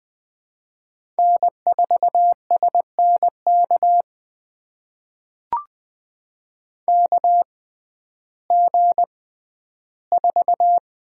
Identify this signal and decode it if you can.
Morse code, heard as a keyed tone, about 20 wpm, spaced as for 12 wpm, 700 Hz. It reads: N4SNK KG4
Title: USA Call Signs - 6x Character Spacing - 20wpm